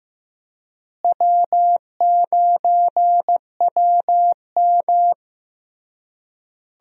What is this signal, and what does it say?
Morse code, heard as a keyed tone, about 16 wpm, 700 Hz. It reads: W9WM
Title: USA Call Signs - 4 Characters - Repeated 3x - 15wpm